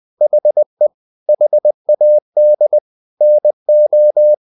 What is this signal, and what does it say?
Morse code, heard as a keyed tone, about 20 wpm, 600 Hz. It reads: HE HAD NO
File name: the old man and the sea / the old man and the sea part 1 20wpm